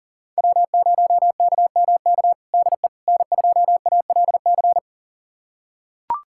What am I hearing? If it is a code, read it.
Morse code, heard as a keyed tone, about 37 wpm, 700 Hz. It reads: W0KMK DE N2ALC
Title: USA Callsign DE Callsign - 5 Characters - 40wpm